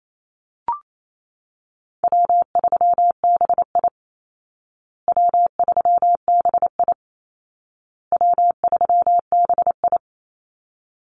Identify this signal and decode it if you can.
Morse code, heard as a keyed tone, about 28 wpm, 700 Hz. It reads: W3BI W3BI W3BI
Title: Easy Call Signs Repeated 3x - 28wpm